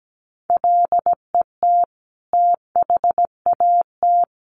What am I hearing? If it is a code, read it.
Morse code, heard as a keyed tone, about 17 wpm, 700 Hz. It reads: LET THAT
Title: Sentences from Top 400 Words 17wpm